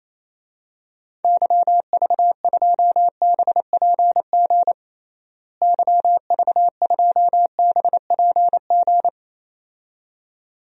Morse code, heard as a keyed tone, about 28 wpm, 700 Hz. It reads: YV2BPG YV2BPG
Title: DXCC-100 Call Signs - 6 Characters - Repeated 2x - 28wpm